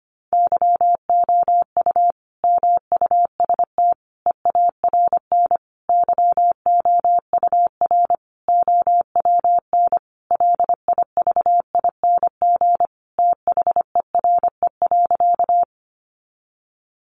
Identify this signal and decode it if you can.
Morse code, heard as a keyed tone, about 25 wpm, 700 Hz. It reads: YOU MUST EARN YOUR OWN LIVING THERE.
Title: Sentences from Top 1500 Words 25wpm